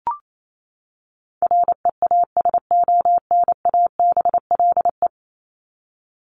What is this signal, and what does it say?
Morse code, heard as a keyed tone, about 28 wpm, 700 Hz. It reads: REASONABLE
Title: Top 1501-1750 Words 28wpm